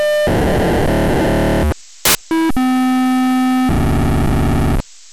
POCSAG pager data.
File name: POCSAG-1200-512